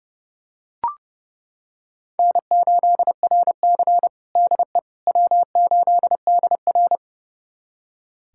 Morse code, heard as a keyed tone, about 32 wpm, 700 Hz. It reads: N8RC DE W8DR
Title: USA Callsign DE Callsign - 4 Characters - 30wpm